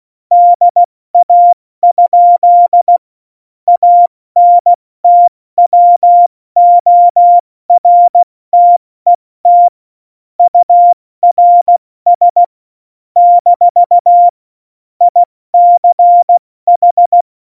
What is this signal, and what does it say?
Morse code, heard as a keyed tone, about 16 wpm, 700 Hz. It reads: DA? ANTWORTET URS - ICH
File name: Morsekurs / Witze / Witz 17. Oktober 17, Tempo 70